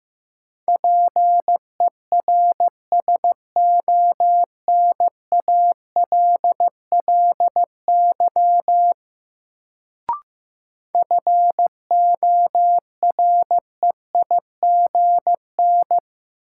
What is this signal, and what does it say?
Morse code, heard as a keyed tone, about 15 wpm, 700 Hz. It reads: PERSONALLY FOREIGN
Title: Sentences from Top 1400 Words - New Word Drill - 15wpm